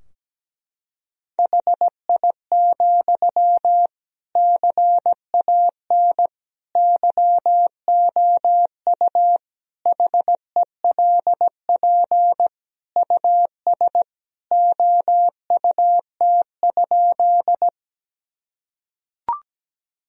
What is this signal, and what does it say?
Morse code, heard as a keyed tone, about 17 wpm, 700 Hz. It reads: HI, CAN YOU HELP US OUT?
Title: Sentences from Top 1000 Words - Review Part 3 - 17wpm